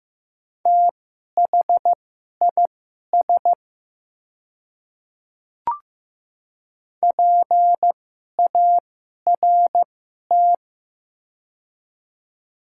Morse code, heard as a keyed tone, about 15 wpm, spaced as for 11 wpm, 700 Hz. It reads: THIS PART
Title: 2x Character Spacing - Top 300 Words - 15wpm